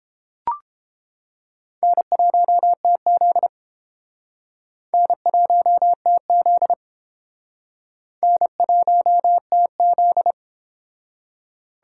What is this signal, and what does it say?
Morse code, heard as a keyed tone, about 30 wpm, 700 Hz. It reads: N1TZ N1TZ N1TZ